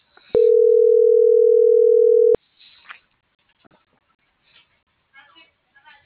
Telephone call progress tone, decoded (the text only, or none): ringback tone